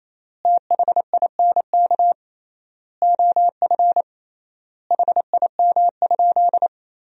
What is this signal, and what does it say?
Morse code, heard as a keyed tone, about 28 wpm, 700 Hz. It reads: THINK OF HIM?
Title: 3x Word-Spacing - Sentences from Top 200 Words - 28wpm